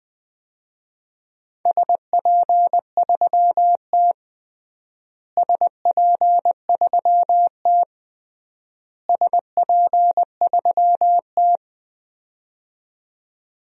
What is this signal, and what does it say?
Morse code, heard as a keyed tone, about 19 wpm, 700 Hz. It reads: SP3T SP3T SP3T
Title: DXCC-100 Call Signs - 4 Characters - Repeated 3x - 20wpm